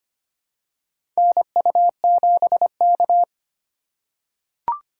Morse code, heard as a keyed tone, about 25 wpm, 700 Hz. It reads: NU7K
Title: USA Call Signs - 4 Characters - 3x Character Spacing - 25wpm